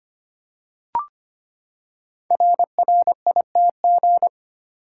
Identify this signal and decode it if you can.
Morse code, heard as a keyed tone, about 25 wpm, 700 Hz. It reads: RRITG
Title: Code Groups - 5 Letter:Numbers - 25wpm